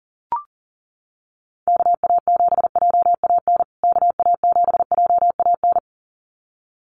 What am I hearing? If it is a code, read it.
Morse code, heard as a keyed tone, about 40 wpm, 700 Hz. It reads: KA7JAN KA7JAN
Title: USA Call Signs Repeated 2x 40wpm